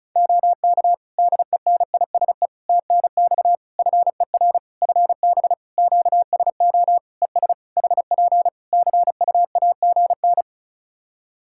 Morse code, heard as a keyed tone, about 34 wpm, 700 Hz. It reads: OK DENISE TNX FER FB QSO ES HP CUAGN
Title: Practice QSO - Exchange 4 - Ending - 35wpm